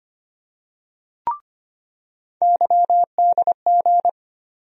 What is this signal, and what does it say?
Morse code, heard as a keyed tone, about 25 wpm, 700 Hz. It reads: YDG
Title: USA Call Sign Postfixes - 3 Letters 25wpm